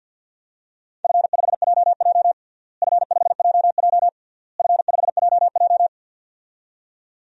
Morse code, heard as a keyed tone, about 53 wpm, 700 Hz. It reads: F5JJ F5JJ F5JJ
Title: DXCC-100 Call Signs - 4 Characters - Repeated 3x - 50wpm